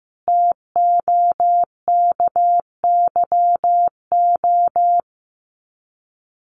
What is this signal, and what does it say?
Morse code, heard as a keyed tone, about 15 wpm, 700 Hz. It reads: TOKYO
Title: Common QSO Phrases 15wpm